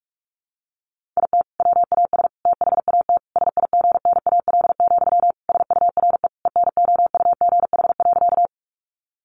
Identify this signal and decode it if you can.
Morse code, heard as a keyed tone, about 44 wpm, 700 Hz. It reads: IT WAS THAT SIGNAL, SURE ENOUGH.
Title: Sentences from Top 1400 Words 45wpm